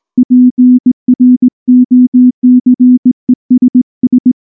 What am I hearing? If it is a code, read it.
Morse code, heard as a keyed tone, about 19 wpm, 250 Hz. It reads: PROCESS
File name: morse / morse-training-20WPM-7